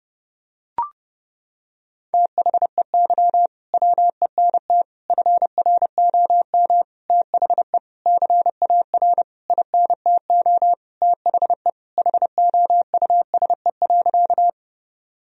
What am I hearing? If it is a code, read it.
Morse code, heard as a keyed tone, about 32 wpm, 700 Hz. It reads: THEY WENT FROM THE CAR INTO THE HOUSE.
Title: Sentences from Top 1000 Words - Review Part 1 - 30wpm